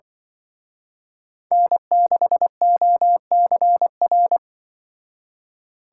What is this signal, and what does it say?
Morse code, heard as a keyed tone, about 25 wpm, 700 Hz. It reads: N6OCR
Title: Speed Racing - USA 1x3 Call Signs 20wpm